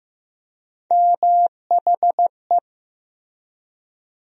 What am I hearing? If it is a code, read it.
Morse code, heard as a keyed tone, about 15 wpm, 700 Hz. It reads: MHE
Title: USA Call Sign Postfixes - 3 Letters 15wpm